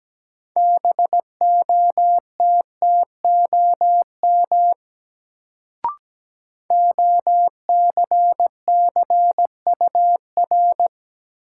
Morse code, heard as a keyed tone, about 17 wpm, 700 Hz. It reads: BOTTOM OCCUR
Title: Sentences from Top 900 Words - New Word Drill - 17wpm